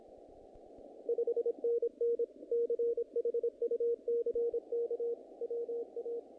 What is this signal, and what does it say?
Morse code, heard as a keyed tone, about 26 wpm, 470 Hz. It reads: 5NN CHUCK WA
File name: contests / naqp-cw-k5tr-op-k5pi- -1900